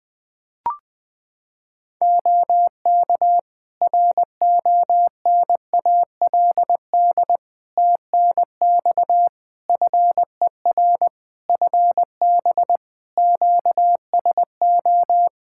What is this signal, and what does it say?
Morse code, heard as a keyed tone, about 20 wpm, 700 Hz. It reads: OK RONALD TNX FER FB QSO
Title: Practice QSO - Exchange 4 - Ending - 20wpm